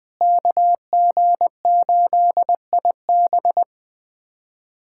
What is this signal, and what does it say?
Morse code, heard as a keyed tone, about 19 wpm, 700 Hz. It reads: KG8IB
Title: USA Callsign DE Callsign - 5 Characters - 20wpm